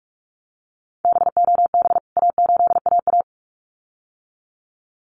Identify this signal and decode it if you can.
Morse code, heard as a keyed tone, about 44 wpm, 700 Hz. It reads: BOB A8AA
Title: CWT Exchange 45wpm